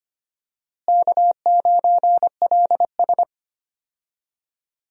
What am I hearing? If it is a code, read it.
Morse code, heard as a keyed tone, about 25 wpm, 700 Hz. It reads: K9LS
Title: USA Call Signs - 4 Characters - Repeated 2x - 25wpm